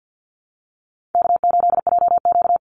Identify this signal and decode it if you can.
Morse code, heard as a keyed tone, about 49 wpm, 700 Hz. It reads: K8JQ
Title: Easy Call Signs Repeated 3x - 50wpm